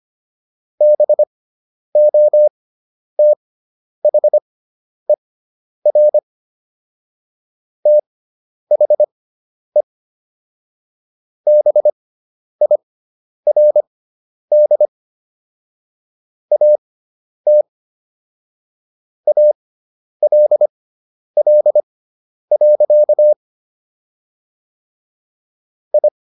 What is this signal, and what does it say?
Morse code, heard as a keyed tone, about 25 wpm, spaced as for 10 wpm, 600 Hz. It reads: BOTHER THE BIRD AT ALL. I